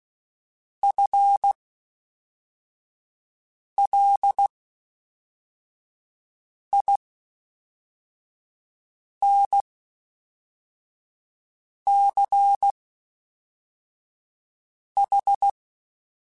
Morse code, heard as a keyed tone, about 16 wpm, 800 Hz. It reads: F L I N C H
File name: Special INGs / Special INGs-mp3s / Fast ing @10